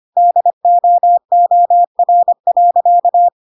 Morse code, heard as a keyed tone, about 25 wpm, 700 Hz. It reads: DOOR.
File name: Pinocchio / Pinocchio ch3 25 25 ik1hge